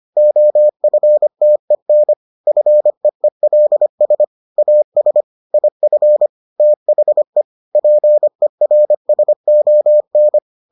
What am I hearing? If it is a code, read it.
Morse code, heard as a keyed tone, about 25 wpm, 600 Hz. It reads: OFTEN FEELS AS IF THE PERSON